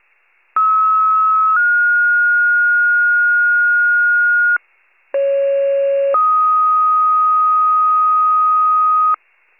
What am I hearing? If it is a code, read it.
Two-tone page, heard as a one-second tone, then a three-second tone, about 1290 then 1430 Hz, about 550 then 1250 Hz. It is heard from 0.6 to 9.1 s.
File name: pager / 73 FIRE / wcalert